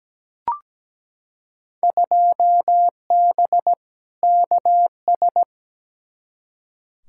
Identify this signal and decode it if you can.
Morse code, heard as a keyed tone, about 17 wpm, 700 Hz. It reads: 2B KS
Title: Field Day Exchanges Encore - 17wpm